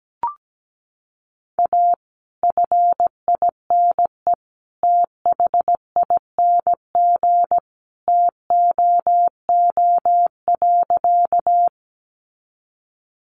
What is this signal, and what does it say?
Morse code, heard as a keyed tone, about 17 wpm, 700 Hz. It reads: A FINE THING TOO.
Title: Sentences from Top 500 Words 17wpm